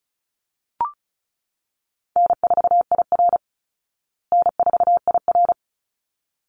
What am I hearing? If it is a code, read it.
Morse code, heard as a keyed tone, about 34 wpm, 700 Hz. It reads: N4IR N4IR
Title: Easy Call Signs Repeated 2x - 35wpm